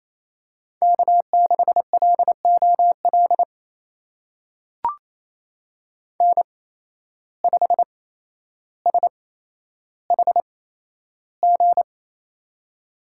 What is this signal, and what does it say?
Morse code, heard as a keyed tone, about 28 wpm, spaced as for 12 wpm, 700 Hz. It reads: K6LOL N5SHG
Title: USA Call Signs - 5 Characters - 8x Character Spacing - 28wpm